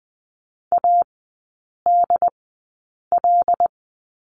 Morse code, heard as a keyed tone, about 20 wpm, spaced as for 8 wpm, 700 Hz. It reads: ADL
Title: 074 Lesson 13e - Letter L - Three Characters 20wpm